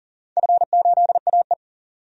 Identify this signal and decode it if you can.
Morse code, heard as a keyed tone, about 40 wpm, 700 Hz. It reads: F8AE
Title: DXCC-100 Call Signs - 4 Characters - 40wpm